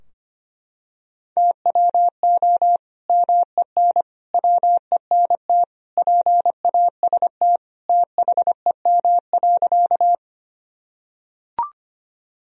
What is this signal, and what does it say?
Morse code, heard as a keyed tone, about 25 wpm, 700 Hz. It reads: TWO MEN WENT PAST THEM.
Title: Sentences from Top 1000 Words - Review Part 3 - 25wpm